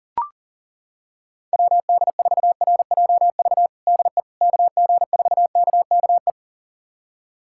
Morse code, heard as a keyed tone, about 40 wpm, 700 Hz. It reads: WD4RJV DE KG4KKE
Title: USA Callsign DE Callsign - 6 Characters - 40wpm